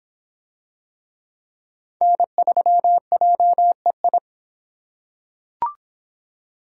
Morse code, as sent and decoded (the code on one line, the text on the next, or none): -. ...-- .--- . ..
N3JEI